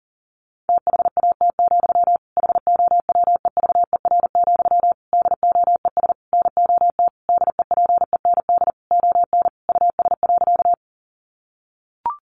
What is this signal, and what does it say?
Morse code, heard as a keyed tone, about 40 wpm, 700 Hz. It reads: THAT, HOWEVER, DOES NOT DEPEND ON US.
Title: Sentences from Top 1300 Words 40wpm